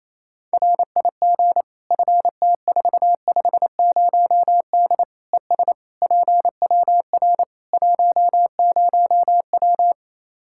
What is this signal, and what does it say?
Morse code, heard as a keyed tone, about 28 wpm, 700 Hz. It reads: RIG FT450D ES PWR 10W
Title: Practice QSO - Exchange 2 - Info - 28wpm